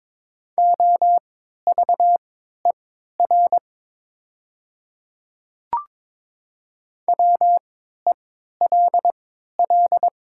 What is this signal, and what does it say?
Morse code, heard as a keyed tone, about 22 wpm, spaced as for 12 wpm, 700 Hz. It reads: OVER WELL
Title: 3x Character Spacing - Top 200 Words - 22wpm